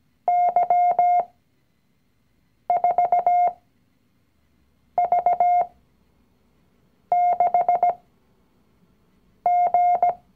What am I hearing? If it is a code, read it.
Morse code, heard as a keyed tone, about 17 wpm, spaced as for 5 wpm, 710 Hz. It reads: Y4V6G
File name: CW-Online-Kurs / Anfänger / Lektion 019